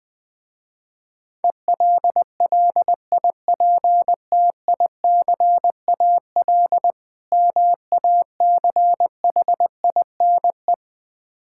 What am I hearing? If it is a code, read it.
Morse code, heard as a keyed tone, about 20 wpm, 700 Hz. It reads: ELLIPTICAL MACHINE
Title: Difficult Words - Speed-Racing with Hints - 20wpm